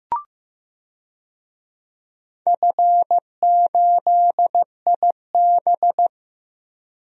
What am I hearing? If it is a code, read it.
Morse code, heard as a keyed tone, about 15 wpm, 700 Hz. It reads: F8IB
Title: DXCC-100 Call Signs - 4 Characters - Repeated 3x - 15wpm